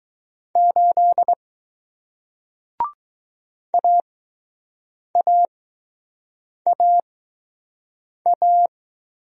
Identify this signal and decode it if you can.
Morse code, heard as a keyed tone, about 21 wpm, spaced as for 6 wpm, 700 Hz. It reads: 8 AAAA